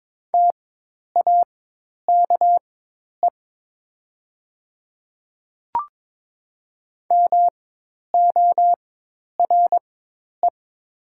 Morse code, heard as a keyed tone, about 22 wpm, spaced as for 10 wpm, 700 Hz. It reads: TAKE MORE
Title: 4x Character Spacing - Top 200 Words - 22wpm